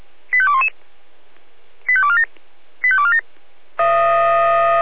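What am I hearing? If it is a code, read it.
Five-tone selective call (ZVEI), heard as quick runs of five tones, tones of about 70 ms. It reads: 85211 85247 85247
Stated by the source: ZVEI five-tone selective call 85247 received on an SDR.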